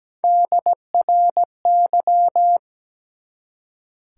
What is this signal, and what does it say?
Morse code, heard as a keyed tone, about 17 wpm, 700 Hz. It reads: DRY
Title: Three Letter Words - 17wpm